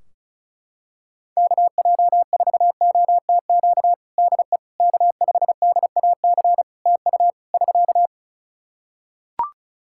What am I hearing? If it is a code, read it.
Morse code, heard as a keyed tone, about 34 wpm, 700 Hz. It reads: KJ4OTQ DE K5DAC TU <SK>
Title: Practice QSO - Exchange 4 - Ending - 35wpm